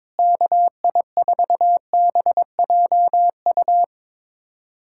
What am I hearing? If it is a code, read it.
Morse code, heard as a keyed tone, about 22 wpm, 700 Hz. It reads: KI4BJU
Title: USA Callsign DE Callsign 22wpm